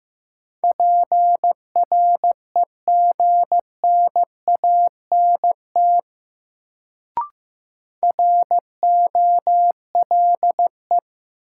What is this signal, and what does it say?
Morse code, heard as a keyed tone, about 15 wpm, 700 Hz. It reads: PREGNANT ROLE
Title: Sentences from Top 1000 Words - Review Part 6 - New Word Drill - 15wpm